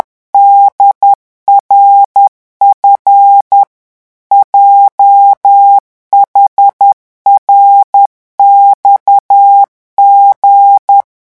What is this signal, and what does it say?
Morse code, heard as a keyed tone, about 10 wpm, 800 Hz. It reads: DRF JHRXG